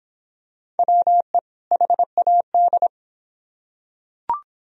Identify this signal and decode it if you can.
Morse code, heard as a keyed tone, about 23 wpm, 700 Hz. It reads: WE HAD